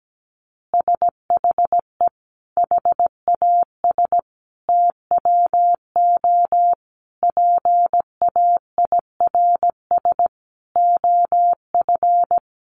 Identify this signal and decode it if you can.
Morse code, heard as a keyed tone, about 17 wpm, 700 Hz. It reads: SHE HAS TWO PAIRS OF
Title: Additional Easy Phrases 17wpm